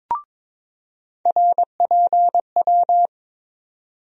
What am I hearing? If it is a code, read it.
Morse code, heard as a keyed tone, about 22 wpm, 700 Hz. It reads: RPW
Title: USA Call Sign Postfixes - 3 Letters 22wpm